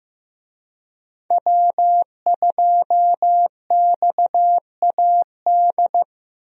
Morse code, heard as a keyed tone, about 15 wpm, 700 Hz. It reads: W2XAD
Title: USA Call Signs - 5 Characters - Repeated 3x - 15wpm